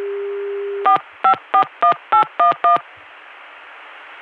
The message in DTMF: 4541811